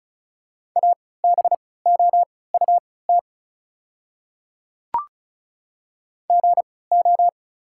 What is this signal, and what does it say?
Morse code, heard as a keyed tone, about 36 wpm, spaced as for 20 wpm, 690 Hz. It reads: ABOUT GO
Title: 3x Character Spacing - Top 200 Words - 35wpm